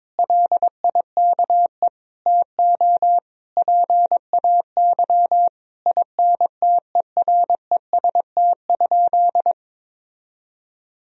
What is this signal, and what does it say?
Morse code, heard as a keyed tone, about 21 wpm, 700 Hz. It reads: LIKE TO PAY INTEREST?
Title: Sentences from Top 1000 Words - Review Part 2 - 22wpm